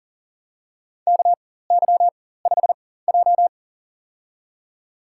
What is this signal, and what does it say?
Morse code, heard as a keyed tone, about 40 wpm, spaced as for 19 wpm, 700 Hz. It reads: KY5J
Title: USA Call Signs - 4 Characters - 4x Character Spacing - 40wpm